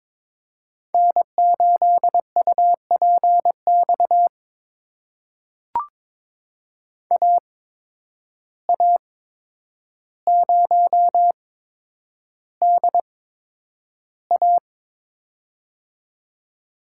Morse code, heard as a keyed tone, about 21 wpm, spaced as for 9 wpm, 700 Hz. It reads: N8UPX AA0DA
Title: USA Call Signs - 5 Characters - 8x Character Spacing - 22wpm